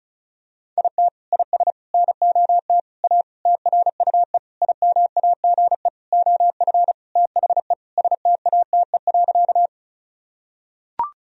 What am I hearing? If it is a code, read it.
Morse code, heard as a keyed tone, about 36 wpm, 700 Hz. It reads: IT IS NOT A TRUE IMAGE OF THE STATE.
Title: Sentences from Top 1000 Words - Review Part 4 - 35wpm